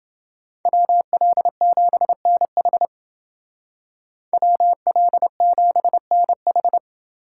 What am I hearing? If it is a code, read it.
Morse code, heard as a keyed tone, about 29 wpm, 700 Hz. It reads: WL7NH WL7NH